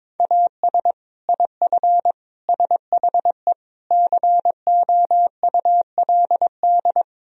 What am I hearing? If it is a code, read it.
Morse code, heard as a keyed tone, about 22 wpm, 700 Hz. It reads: AS IF SHE COULD
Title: Sentences from Top 1000 Words - Review Part 1 - 22wpm